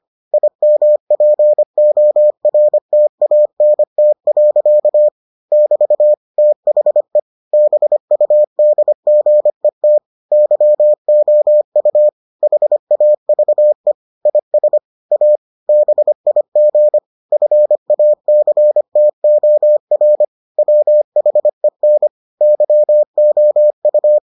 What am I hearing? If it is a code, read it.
Morse code, heard as a keyed tone, about 25 wpm, 600 Hz. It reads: IMPORTANT. <BT> THE BUDGET YOU HAVE IS A BIG FACTOR WHEN YOU